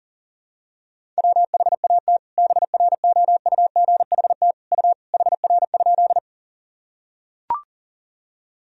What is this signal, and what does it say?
Morse code, heard as a keyed tone, about 40 wpm, 700 Hz. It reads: WHAT BROUGHT U HR?
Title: Sentences from Top 500 Words and Abbreviations 40wpm